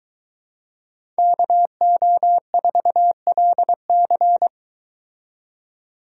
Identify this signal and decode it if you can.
Morse code, heard as a keyed tone, about 23 wpm, 700 Hz. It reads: KO4LC